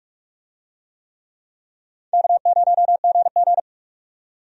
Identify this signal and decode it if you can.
Morse code, heard as a keyed tone, about 42 wpm, 700 Hz. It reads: K0GG
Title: USA Call Signs - 4 Characters - 45wpm